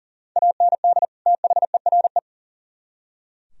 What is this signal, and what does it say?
Morse code, heard as a keyed tone, about 49 wpm, 700 Hz. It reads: AND THERE
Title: Speed-Racing - Sets of 2 Words 40wpm